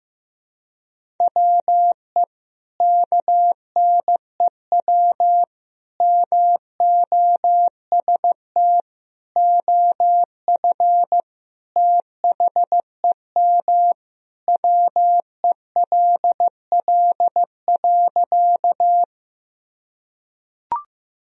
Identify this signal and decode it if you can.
Morse code, heard as a keyed tone, about 16 wpm, 700 Hz. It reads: WE KNEW MOST OF THEM WELL.
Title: Sentences from Top 1000 Words - Review Part 1 - 15wpm